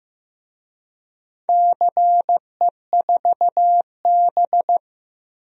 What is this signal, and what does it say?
Morse code, heard as a keyed tone, about 15 wpm, 700 Hz. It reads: CE4B
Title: DXCC-100 Call Signs - 4 Characters - 15wpm